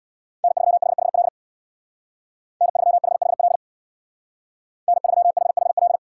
Morse code, heard as a keyed tone, about 64 wpm, 700 Hz. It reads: N3HHL N3HHL N3HHL